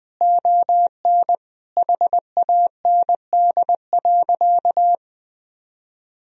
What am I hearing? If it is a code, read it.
Morse code, heard as a keyed tone, about 20 wpm, 700 Hz. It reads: ON HAND.
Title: Sentences from Top 1000 Words - Review Part 4 - 20wpm